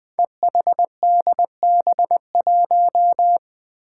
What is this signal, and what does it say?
Morse code, heard as a keyed tone, about 20 wpm, 700 Hz. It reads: EHDB1
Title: Five Challenging Letter-Number Combinations 20wpm